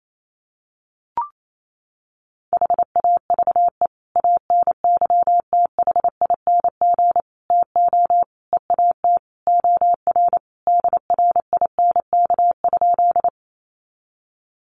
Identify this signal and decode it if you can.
Morse code, heard as a keyed tone, about 28 wpm, 700 Hz. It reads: HAVE ANYTHING TO EAT OR DRINK?
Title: Sentences from Top 1500 Words 28wpm